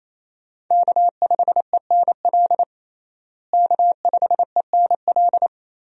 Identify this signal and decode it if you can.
Morse code, heard as a keyed tone, about 28 wpm, 700 Hz. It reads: K5ENL K5ENL
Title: USA Call Signs - 5 Characters - Repeated 3x - 28wpm